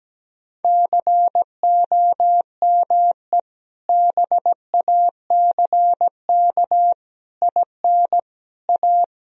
Morse code, heard as a keyed tone, about 17 wpm, 700 Hz. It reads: COME BACK IN A
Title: Sentences from Top 1000 Words - Review Part 3 - 17wpm